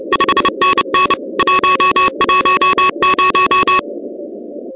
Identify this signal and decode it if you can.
Morse code, heard as a keyed tone, about 30 wpm, 1110 Hz, with noise. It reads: 5NN 110
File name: contests / 2019-cw-wpx-kt5j-k5pi- -2330 0-16